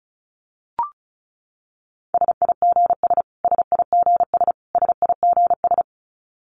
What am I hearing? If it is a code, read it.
Morse code, heard as a keyed tone, about 34 wpm, 700 Hz. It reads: SIGS SIGS SIGS
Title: Abbreviations Repeated 3x - 35wpm